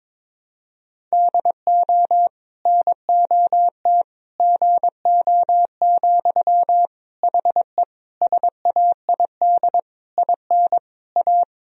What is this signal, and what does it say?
Morse code, heard as a keyed tone, about 22 wpm, 700 Hz. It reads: DO NOT GO, HE SAID IN A
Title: Sentences from Top 1000 Words - Review Part 1 - 22wpm